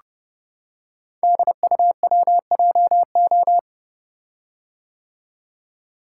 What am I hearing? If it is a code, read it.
Morse code, heard as a keyed tone, about 30 wpm, 700 Hz. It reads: DUWJO
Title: Code Groups - 5 Letter:Numbers - 30wpm